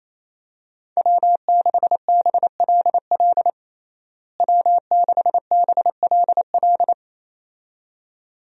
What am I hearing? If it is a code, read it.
Morse code, heard as a keyed tone, about 28 wpm, 700 Hz. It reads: W6BLL W6BLL
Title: USA Call Signs - 5 Characters - Repeated 2x - 28wpm